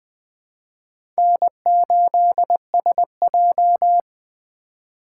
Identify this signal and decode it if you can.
Morse code, heard as a keyed tone, about 20 wpm, 700 Hz. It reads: N8SJ
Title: USA Call Signs - 4 Characters - 8x Character Spacing - 20wpm